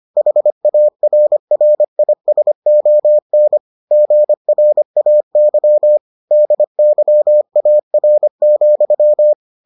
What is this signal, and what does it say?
Morse code, heard as a keyed tone, about 25 wpm, 600 Hz. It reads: HARRISON GRAY DYAR,